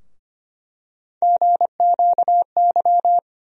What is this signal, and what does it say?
Morse code, heard as a keyed tone, about 25 wpm, 700 Hz. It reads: GQY
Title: USA Call Sign Postfixes - 3 Letters 25wpm